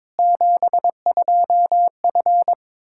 Morse code, heard as a keyed tone, about 21 wpm, 700 Hz. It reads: 72F
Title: Practice QSO - Exchange 2 - Info - 22wpm